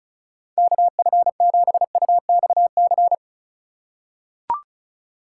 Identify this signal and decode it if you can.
Morse code, heard as a keyed tone, about 36 wpm, 700 Hz. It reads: KF7UXC
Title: K1USN SST QSO Practice - 35wpm